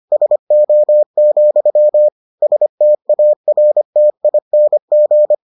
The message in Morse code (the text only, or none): SO, STARTING